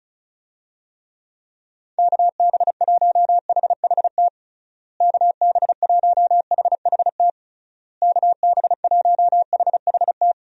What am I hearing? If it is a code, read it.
Morse code, heard as a keyed tone, about 36 wpm, 700 Hz. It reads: KB1HHT KB1HHT KB1HHT
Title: USA Call Signs - 6 Characters - Repeated 3x - 35wpm